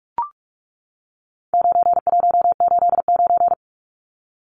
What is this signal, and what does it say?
Morse code, heard as a keyed tone, about 44 wpm, 700 Hz. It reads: 9189
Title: Numbers - 4-Digit 45wpm